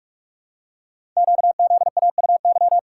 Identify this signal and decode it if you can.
Morse code, heard as a keyed tone, about 44 wpm, 700 Hz. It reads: QZAUY
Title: Code Groups - 5 Letters - 45wpm